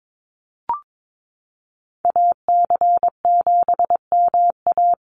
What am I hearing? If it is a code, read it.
Morse code, heard as a keyed tone, about 22 wpm, 700 Hz. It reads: AC7MA
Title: NA SOTA Activator Callsigns 22wpm